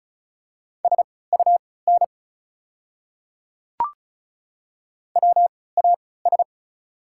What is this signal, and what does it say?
Morse code, heard as a keyed tone, about 36 wpm, spaced as for 20 wpm, 700 Hz. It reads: SUN WAS
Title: 3x Character Spacing - Top 300 Words - 35wpm